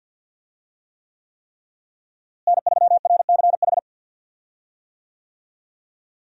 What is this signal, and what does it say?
Morse code, heard as a keyed tone, about 52 wpm, 700 Hz. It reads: N3RCH
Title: USA Call Signs - 5 Characters - Repeated 3x - 50wpm